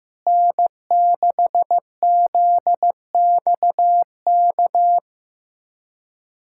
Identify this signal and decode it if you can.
Morse code, heard as a keyed tone, about 16 wpm, 700 Hz. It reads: N6ZXK
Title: USA Callsign DE Callsign - 5 Characters - 15wpm